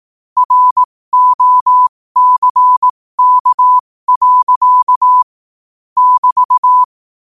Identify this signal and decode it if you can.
Morse code, heard as a keyed tone, about 18 wpm, spaced as for 15 wpm, 1000 Hz. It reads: ROCK. <BT>
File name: 15wpm